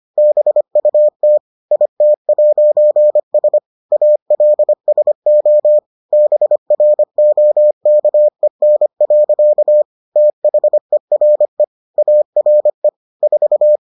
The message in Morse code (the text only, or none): BUT IT'S ALSO BROKEN. THERE ARE 4